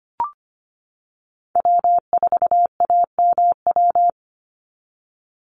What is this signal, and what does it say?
Morse code, heard as a keyed tone, about 25 wpm, 700 Hz. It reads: W4AMW
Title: NA SOTA Chaser Callsigns 25wpm